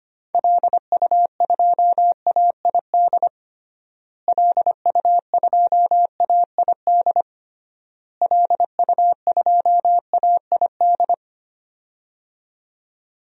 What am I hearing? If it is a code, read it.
Morse code, heard as a keyed tone, about 25 wpm, 700 Hz. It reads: LU2AID LU2AID LU2AID
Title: DXCC-100 Call Signs - 6 Characters - Repeated 3x - 25wpm